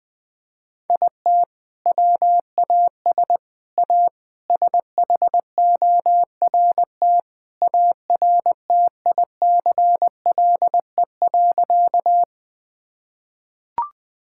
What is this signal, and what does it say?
Morse code, heard as a keyed tone, about 20 wpm, 700 Hz. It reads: IT WAS A SHORT ARTICLE.
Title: Sentences from Top 1000 Words - Review Part 5 - 20wpm